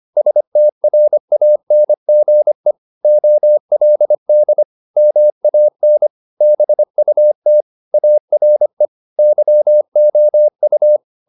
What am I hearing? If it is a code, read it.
Morse code, heard as a keyed tone, about 25 wpm, 600 Hz. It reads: STRANGE OLD MAN BUT ARE YOU
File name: the old man and the sea part 1 25wpm